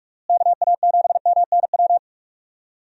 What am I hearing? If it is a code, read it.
Morse code, heard as a keyed tone, about 46 wpm, 700 Hz. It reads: KA7MNW
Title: USA Call Signs - 6 Characters - Repeated 3x - 45wpm